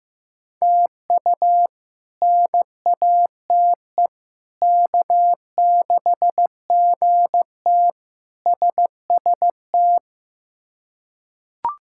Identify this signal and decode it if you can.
Morse code, heard as a keyed tone, about 15 wpm, 700 Hz. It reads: TU NATE K6GT SST
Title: K1USN SST QSO Practice - 15wpm